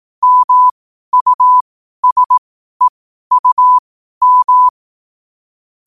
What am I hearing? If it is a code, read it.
Morse code, heard as a keyed tone, about 18 wpm, spaced as for 12 wpm, 1000 Hz. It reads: MUSEUM
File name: qotdfiles / 12wpm